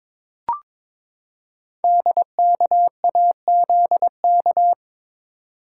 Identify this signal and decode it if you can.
Morse code, heard as a keyed tone, about 23 wpm, 700 Hz. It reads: DKAZK
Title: Code Groups - 5 Letters - 22wpm